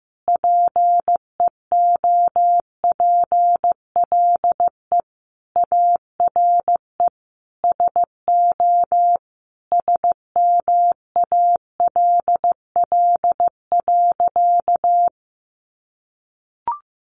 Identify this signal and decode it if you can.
Morse code, heard as a keyed tone, about 15 wpm, 700 Hz. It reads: PEOPLE ARE SO SMALL.
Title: Additional Easy Phrases 15wpm